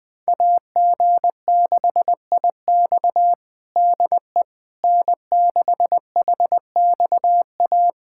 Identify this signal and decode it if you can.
Morse code, heard as a keyed tone, about 21 wpm, 700 Hz. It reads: AG6IX DE N6HXA
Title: USA Callsign DE Callsign - 5 Characters - 20wpm